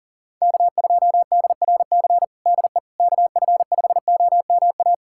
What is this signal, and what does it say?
Morse code, heard as a keyed tone, about 40 wpm, 700 Hz. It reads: K2DRC DE KF5OMA
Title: USA Callsign DE Callsign 40wpm